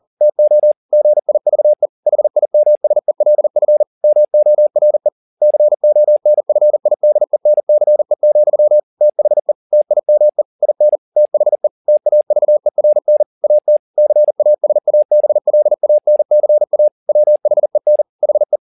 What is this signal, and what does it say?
Morse code, heard as a keyed tone, about 40 wpm, 600 Hz. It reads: TO GIVE HIMSELF MORE CONFIDENCE, THE TIME IN THE TAVERN AT CASABLANCA WHEN HE